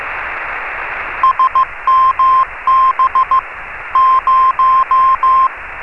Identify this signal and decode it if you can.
Morse code, heard as a keyed tone, about 15 wpm, 1050 Hz. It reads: SMB 0
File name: RS30-CW-Beacon